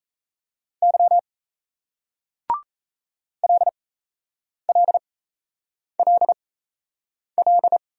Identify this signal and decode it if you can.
Morse code, heard as a keyed tone, about 37 wpm, spaced as for 8 wpm, 700 Hz. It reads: Y LLLL